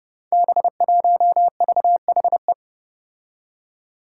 Morse code, heard as a keyed tone, about 28 wpm, 700 Hz. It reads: B1VHE
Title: Five Challenging Letter-Number Combinations 30wpm